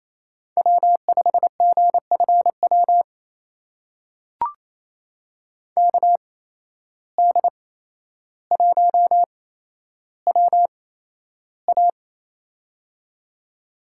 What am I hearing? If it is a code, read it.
Morse code, heard as a keyed tone, about 28 wpm, spaced as for 12 wpm, 700 Hz. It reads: W5GFW KD1WA
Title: USA Call Signs - 5 Characters - 8x Character Spacing - 28wpm